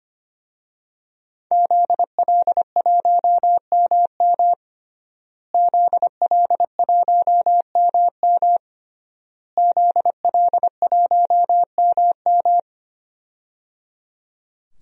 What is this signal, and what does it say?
Morse code, heard as a keyed tone, about 25 wpm, 700 Hz. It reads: ZL1MM ZL1MM ZL1MM
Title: DXCC-100 Call Signs - 5 Characters - Repeated 3x - 25wpm